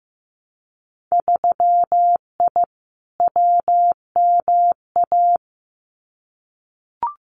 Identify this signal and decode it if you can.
Morse code, heard as a keyed tone, about 15 wpm, 700 Hz. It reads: 3I WMA
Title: Winter Field Day Exchanges Encore 15wpm